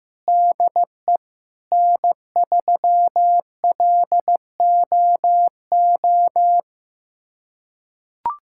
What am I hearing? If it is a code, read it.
Morse code, heard as a keyed tone, about 15 wpm, 700 Hz. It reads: DE N3LOO
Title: USA Callsign DE Callsign - 5 Characters - 15wpm